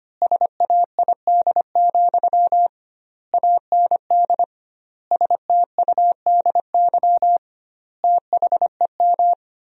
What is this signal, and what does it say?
Morse code, heard as a keyed tone, about 25 wpm, 700 Hz. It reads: SAID, AND STUDY THEM
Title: 2x Word-Spacing - Sentences from Top 300 Words - 25wpm